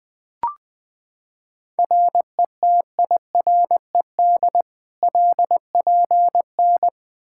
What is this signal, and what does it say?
Morse code, heard as a keyed tone, about 20 wpm, 700 Hz. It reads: RETIRED LPN
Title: Practice QSO - Exchange 3 - More Info - 20wpm